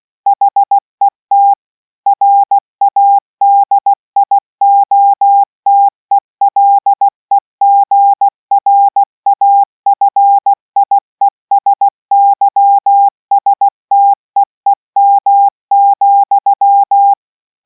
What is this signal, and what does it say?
Morse code, heard as a keyed tone, about 16 wpm, 800 Hz. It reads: HET RADIOTELEGRAFIESYSTEEM,